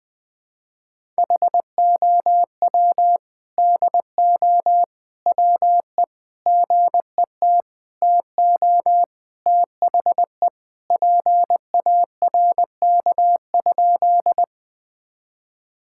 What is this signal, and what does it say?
Morse code, heard as a keyed tone, about 20 wpm, 700 Hz. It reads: HOW DO WE GET TO THE PARK?
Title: Sentences with 4-Word Phrases - 20wpm